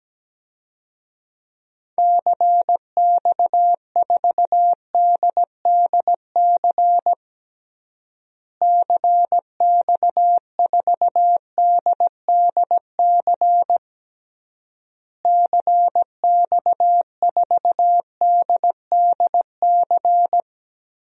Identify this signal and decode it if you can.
Morse code, heard as a keyed tone, about 17 wpm, 700 Hz. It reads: CX4DDC CX4DDC CX4DDC
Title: DXCC-100 Call Signs - 6 Characters - Repeated 3x - 17wpm